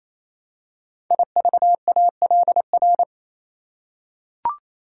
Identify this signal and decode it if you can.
Morse code, heard as a keyed tone, about 27 wpm, 700 Hz. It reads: IVALR
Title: Code Groups - 5 Letters - 28wpm